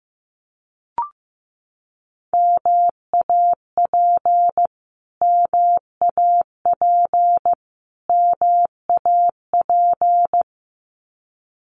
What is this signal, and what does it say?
Morse code, heard as a keyed tone, about 15 wpm, 700 Hz. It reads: MAP MAP MAP
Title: Top 500 Words Repeated 15wpm